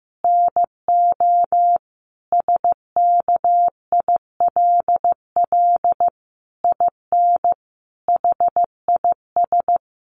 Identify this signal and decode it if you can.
Morse code, heard as a keyed tone, about 15 wpm, 700 Hz. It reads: NO SKILL IN HIS
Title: Sentences from Top 1200 Words 15wpm